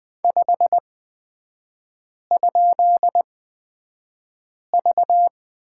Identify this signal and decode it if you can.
Morse code, heard as a keyed tone, about 20 wpm, spaced as for 5 wpm, 690 Hz. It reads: 5?V
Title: 255 Lesson 41e - Period - Three Characters 20wpm